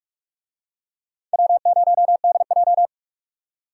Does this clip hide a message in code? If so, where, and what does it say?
Yes: Morse code from 1.3 to 2.9 s, heard as a keyed tone, about 44 wpm, 700 Hz. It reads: W0DJ